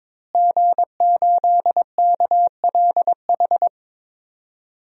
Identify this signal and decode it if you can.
Morse code, heard as a keyed tone, about 22 wpm, 700 Hz. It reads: G8KLH
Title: DXCC-100 Call Signs - 5 Characters - Repeated 3x - 22wpm